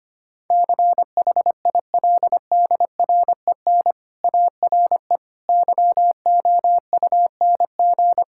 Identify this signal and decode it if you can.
Morse code, heard as a keyed tone, about 25 wpm, 700 Hz. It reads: CHILDREN ARE YOUNG
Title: Sentences from Top 1000 Words - Review Part 6 - 25wpm